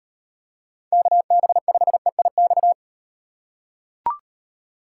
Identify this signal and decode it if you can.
Morse code, heard as a keyed tone, about 38 wpm, 700 Hz. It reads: KB5EIX